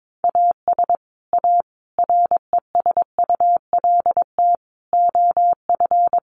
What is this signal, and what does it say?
Morse code, heard as a keyed tone, about 22 wpm, 700 Hz. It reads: AS A RESULT OF
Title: Sets of 5 Words Encore2 22wpm